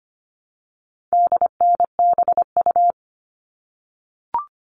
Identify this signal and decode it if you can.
Morse code, heard as a keyed tone, about 25 wpm, 700 Hz. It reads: DNBU
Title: All Cut Numbers - 4-Digit 25wpm